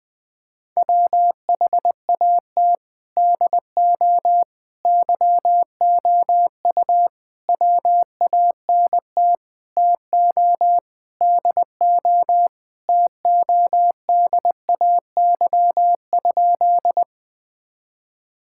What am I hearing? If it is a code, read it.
Morse code, heard as a keyed tone, about 21 wpm, 700 Hz. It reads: WHAT DO YOU WANT TO DO TODAY?
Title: Sentences with 4-Word Phrases - Spoken Beginning - 20wpm